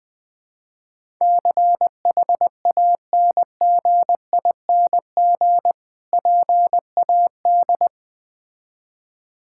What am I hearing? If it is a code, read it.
Morse code, heard as a keyed tone, about 20 wpm, 690 Hz. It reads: CHANGING PAD
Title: Difficult Words with Hints - 20wpm